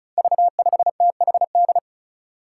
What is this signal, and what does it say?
Morse code, heard as a keyed tone, about 34 wpm, 700 Hz. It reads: V5THD
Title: Five Challenging Letter-Number Combinations 35wpm